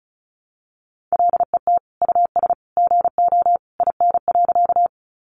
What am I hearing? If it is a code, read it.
Morse code, heard as a keyed tone, about 34 wpm, 700 Hz. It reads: LET US GO IN.
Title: Sentences from Top 300 Words 35wpm